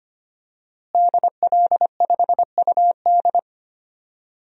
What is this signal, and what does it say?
Morse code, heard as a keyed tone, about 25 wpm, 700 Hz. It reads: DL5UD
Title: DXCC-100 Call Signs - 5 Characters - Repeated 2x - 25wpm